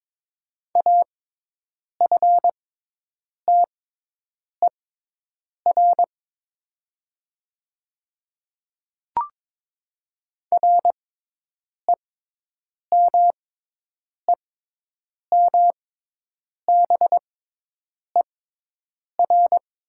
Morse code, heard as a keyed tone, about 22 wpm, spaced as for 8 wpm, 700 Hz. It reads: AFTER REMEMBER
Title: 6x Character Spacing - Top 500 Words - 22wpm